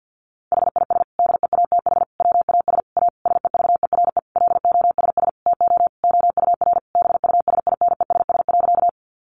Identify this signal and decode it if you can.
Morse code, heard as a keyed tone, about 50 wpm, 700 Hz. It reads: HIS DEATH WAS A SEVERE LOSS TO OUR BUSINESS.
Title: Sentences from Top 1400 Words 50wpm